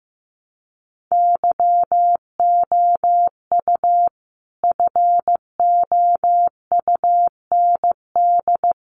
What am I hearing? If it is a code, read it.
Morse code, heard as a keyed tone, about 15 wpm, 700 Hz. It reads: YOU FOUND
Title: Sentences from Top 300 Words 15wpm